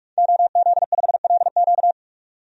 Morse code, heard as a keyed tone, about 41 wpm, 700 Hz. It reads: OZ5LQ
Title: DXCC-100 Call Signs - 5 Characters - 45wpm